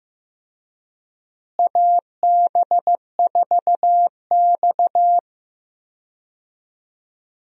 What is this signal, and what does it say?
Morse code, heard as a keyed tone, about 15 wpm, 700 Hz. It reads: AB4X
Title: USA Call Signs - 4 Characters - Repeated 3x - 15wpm